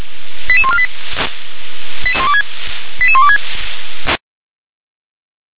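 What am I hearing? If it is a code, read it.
Five-tone selective call (ZVEI), heard as quick runs of five tones, tones of about 70 ms. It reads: 88148 88126
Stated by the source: ZVEI five-tone selective call received on an SDR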